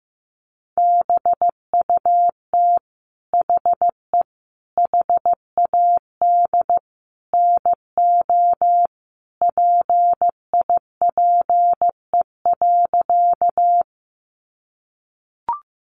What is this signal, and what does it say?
Morse code, heard as a keyed tone, about 15 wpm, 700 Hz. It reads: BUT HE HAD NO PIPE.
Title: Sentences from Top 1500 Words 15wpm